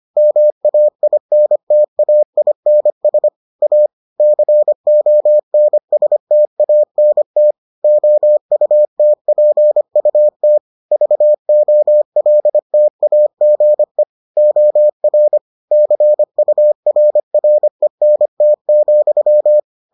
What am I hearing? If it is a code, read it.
Morse code, heard as a keyed tone, about 25 wpm, 600 Hz. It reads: MAINTAINS A CONSTANT OUTPUT VOLTAGE OR CURRENT,